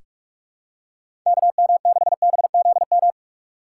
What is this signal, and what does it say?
Morse code, heard as a keyed tone, about 46 wpm, 700 Hz. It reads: KM6BZM
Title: USA Call Signs - 6 Characters - 3x Character Spacing - 45wpm